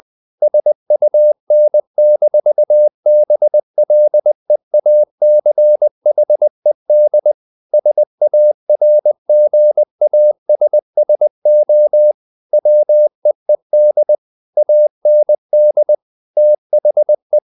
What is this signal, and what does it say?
Morse code, heard as a keyed tone, about 20 wpm, 600 Hz. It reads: SUN-BLEACHED SARGASSO WEED AND THE
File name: the old man and the sea / the old man and the sea part 1 20wpm